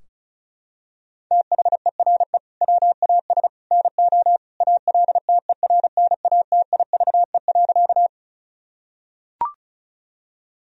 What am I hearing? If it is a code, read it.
Morse code, heard as a keyed tone, about 36 wpm, 700 Hz. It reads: THERE WAS NO ALTERNATIVE.
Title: Sentences from Top 1000 Words - Review Part 4 - 35wpm